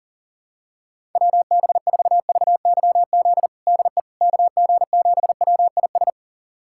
Morse code, heard as a keyed tone, about 40 wpm, 700 Hz. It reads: WB4VYZ DE KG7WIS
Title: USA Callsign DE Callsign 40wpm